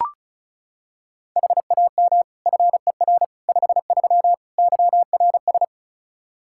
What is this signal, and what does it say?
Morse code, heard as a keyed tone, about 36 wpm, 700 Hz. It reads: HAM FER 53 YRS
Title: Practice QSO - Exchange 3 - More Info - 35wpm